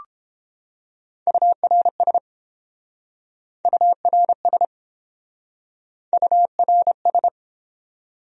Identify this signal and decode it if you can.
Morse code, heard as a keyed tone, about 31 wpm, 700 Hz. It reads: URS URS URS